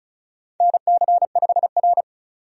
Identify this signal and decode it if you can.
Morse code, heard as a keyed tone, about 36 wpm, 700 Hz. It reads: NC5R
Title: USA Call Signs - 4 Characters - 35wpm